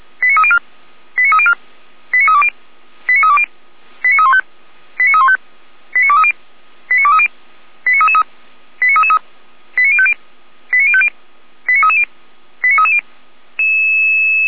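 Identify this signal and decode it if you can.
Five-tone selective call (ZVEI), heard as quick runs of five tones, tones of about 70 ms. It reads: 89304 89304 89320 89320 89315 89315 89230 89230 89333 89333 89050 89050 89330 89330
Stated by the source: ZVEI five-tone call received by SDR, archived by Feuerwehr Orlamünde (fire brigade).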